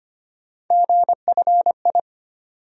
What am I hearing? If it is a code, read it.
Morse code, heard as a keyed tone, about 25 wpm, 700 Hz. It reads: GFI
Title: USA Call Sign Postfixes - 3 Letters 25wpm